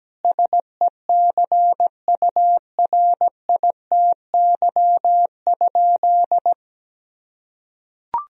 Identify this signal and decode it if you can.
Morse code, heard as a keyed tone, about 17 wpm, 700 Hz. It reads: SECURITY?
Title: Sentences from Top 1000 Words - Review Part 4 - 17wpm